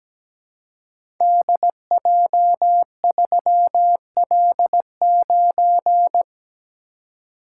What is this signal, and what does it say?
Morse code, heard as a keyed tone, about 17 wpm, 700 Hz. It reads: DJ3L9
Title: Code Groups - 5 Letter:Numbers - 17wpm